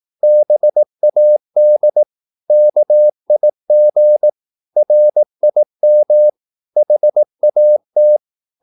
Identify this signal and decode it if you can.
Morse code, heard as a keyed tone, about 18 wpm, 600 Hz. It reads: BAD KIG RIM HAT
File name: Korte ord DK 18wpm